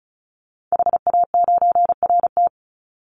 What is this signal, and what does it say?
Morse code, heard as a keyed tone, about 36 wpm, 700 Hz. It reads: HA9RT
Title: Call Signs - 35wpm